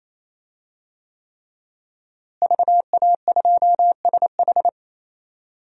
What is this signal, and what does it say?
Morse code, heard as a keyed tone, about 28 wpm, 700 Hz. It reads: VA2SH
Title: DXCC-100 Call Signs - 5 Characters - Repeated 3x - 28wpm